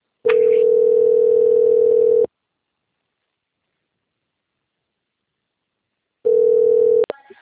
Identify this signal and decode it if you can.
Ringback tone (North America), 440 + 480 Hz.